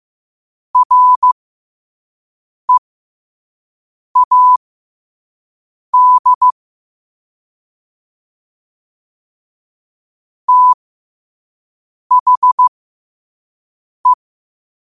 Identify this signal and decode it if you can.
Morse code, heard as a keyed tone, about 15 wpm, spaced as for 5 wpm, 1000 Hz. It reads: READ THE